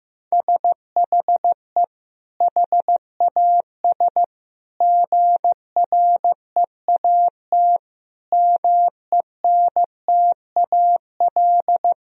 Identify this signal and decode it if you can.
Morse code, heard as a keyed tone, about 15 wpm, 700 Hz. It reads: SHE HAS GREAT MENTAL
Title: Sentences from Top 1000 Words - Review Part 4 - 15wpm